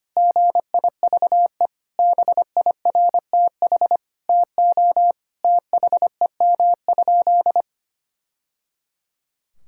Morse code, heard as a keyed tone, about 25 wpm, 700 Hz. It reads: GIVE BIRTH TO THEM?
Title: Sentences from Top 1000 Words - Review Part 5 - 25wpm